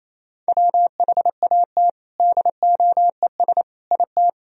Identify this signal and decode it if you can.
Morse code, heard as a keyed tone, about 28 wpm, 700 Hz. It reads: WHAT DOES IT
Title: Sentences from Top 1000 Words - Review Part 4 - 28wpm